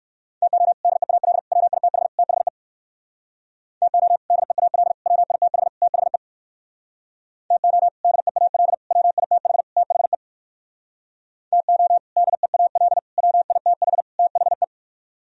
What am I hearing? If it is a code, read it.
Morse code, heard as a keyed tone, about 65 wpm, 700 Hz. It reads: TO DEAL WITH THE TO DEAL WITH THE TO DEAL WITH THE TO DEAL WITH THE